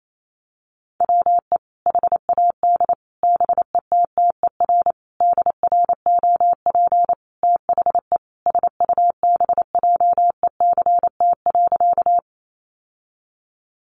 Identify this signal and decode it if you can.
Morse code, heard as a keyed tone, about 28 wpm, 700 Hz. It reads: WE HAD BETTER DROP THE SUBJECT.
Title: Sentences from Top 1400 Words 28wpm